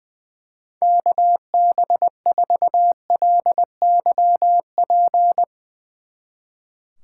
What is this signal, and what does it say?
Morse code, heard as a keyed tone, about 20 wpm, 700 Hz. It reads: KB4LYP
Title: Speed Racing - USA 2x3 Call Signs 20wpm